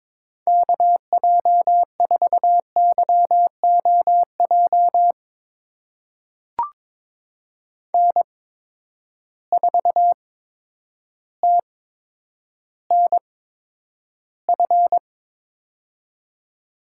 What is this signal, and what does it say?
Morse code, heard as a keyed tone, about 22 wpm, spaced as for 10 wpm, 700 Hz. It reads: KJ4YOJ N4TNF
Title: USA Call Signs - 8x Character Spacing - 22wpm